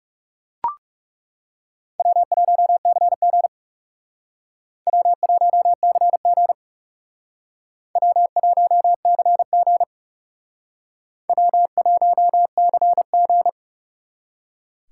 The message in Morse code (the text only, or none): W1CG W1CG W1CG W1CG